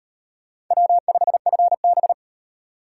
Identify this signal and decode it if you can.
Morse code, heard as a keyed tone, about 36 wpm, 700 Hz. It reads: W5FB